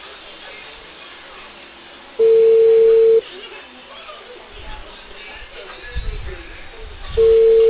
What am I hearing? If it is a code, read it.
Ringback tone (China, 450 Hz), with speech.